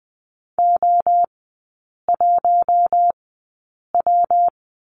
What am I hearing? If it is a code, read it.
Morse code, heard as a keyed tone, about 20 wpm, spaced as for 8 wpm, 700 Hz. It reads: O1W
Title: 114 Lesson 19e - Letter w - Three Characters 20wpm